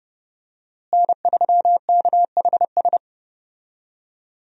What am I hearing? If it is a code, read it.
Morse code, heard as a keyed tone, about 30 wpm, 700 Hz. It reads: N3KHS
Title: USA Call Signs - 5 Characters - Repeated 3x - 30wpm